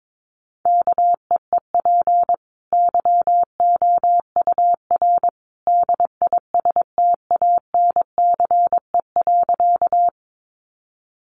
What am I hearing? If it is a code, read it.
Morse code, heard as a keyed tone, about 22 wpm, 700 Hz. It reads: KEEP YOUR DISTANCE.
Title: Sentences from Top 1200 Words 22wpm